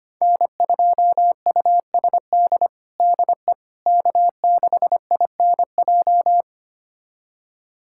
Morse code, heard as a keyed tone, about 25 wpm, 700 Hz. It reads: N2USD DE K6INJ
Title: USA Callsign DE Callsign - 5 Characters - 25wpm